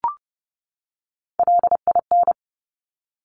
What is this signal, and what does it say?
Morse code, heard as a keyed tone, about 30 wpm, 690 Hz. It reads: LIN
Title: Top Three Letter Combinations in Words 30wpm